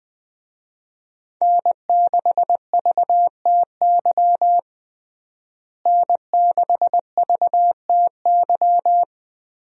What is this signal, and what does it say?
Morse code, heard as a keyed tone, about 21 wpm, 700 Hz. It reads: N6VTY N6VTY
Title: USA Call Signs - 5 Characters - Repeated 3x - 20wpm